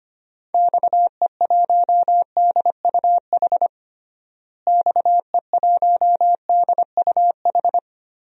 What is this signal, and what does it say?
Morse code, heard as a keyed tone, about 25 wpm, 700 Hz. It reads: XE1DUH XE1DUH
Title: DXCC-100 Call Signs - 6 Characters - Repeated 2x - 25wpm